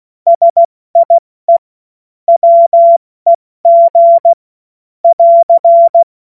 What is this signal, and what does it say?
Morse code, heard as a keyed tone, about 16 wpm, 680 Hz. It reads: SIE WEG <AR>